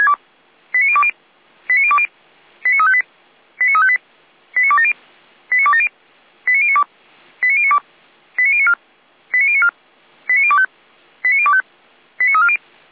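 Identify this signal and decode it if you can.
Five-tone selective call (ZVEI), heard as quick runs of five tones, tones of about 70 ms. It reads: 89020 89020 89358 89358 89270 89270 89092 89092 89094 89094 89025 89025 89340
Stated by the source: ZVEI five-tone selective call received on an SDR.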